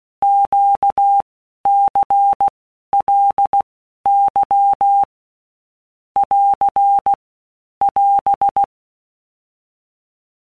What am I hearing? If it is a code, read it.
Morse code, heard as a keyed tone, about 16 wpm, 800 Hz. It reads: Q C L Y <AR> <AS>
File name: Basic Letter Groups By Blocks / Groups@15 / LFCYQ@15